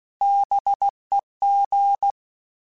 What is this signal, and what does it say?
Morse code, heard as a keyed tone, about 16 wpm, 800 Hz. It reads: BEG
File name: Alphabetical / Alphabetical mp3s / Bs @15